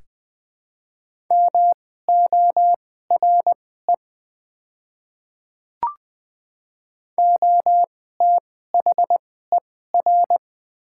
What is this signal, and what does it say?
Morse code, heard as a keyed tone, about 20 wpm, spaced as for 14 wpm, 700 Hz. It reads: MORE OTHER
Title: 2x Character Spacing - Top 100 Words - 20wpm